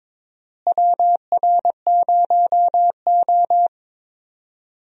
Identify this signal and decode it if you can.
Morse code, heard as a keyed tone, about 22 wpm, 700 Hz. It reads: WR0O
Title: USA Call Signs - 4 Characters - Repeated 3x - 22wpm